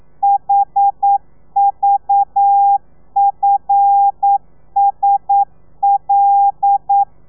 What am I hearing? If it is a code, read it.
Morse code, heard as a keyed tone, about 9 wpm, 800 Hz. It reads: HVFSL
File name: telegrafia / i2tfj / Lez5 40